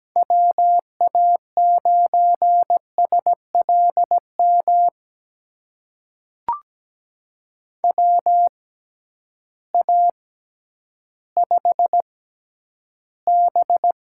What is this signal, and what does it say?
Morse code, heard as a keyed tone, about 17 wpm, spaced as for 10 wpm, 700 Hz. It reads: WA9SLM WA5B
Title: USA Call Signs - 6 Characters - 6x Character Spacing - 17wpm